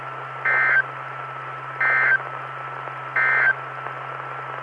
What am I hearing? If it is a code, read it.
Emergency Alert System (SAME) tones, heard as harsh buzzing bursts. It is heard from 0.3 to 3.6 s.
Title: Saline, AK SKYWARN Repeater System - W5RHS